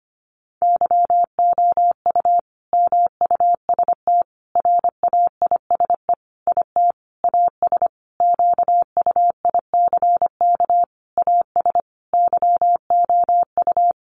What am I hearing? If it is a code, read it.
Morse code, heard as a keyed tone, about 25 wpm, 700 Hz. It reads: YOU MUST RAISE IT AS QUICK AS YOU
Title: Sentences from Top 1400 Words 25wpm